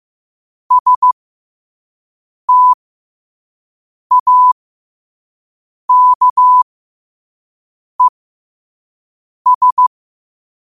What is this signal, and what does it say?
Morse code, heard as a keyed tone, about 15 wpm, spaced as for 5 wpm, 1000 Hz. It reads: STAKES
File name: qotdfiles / 5wpm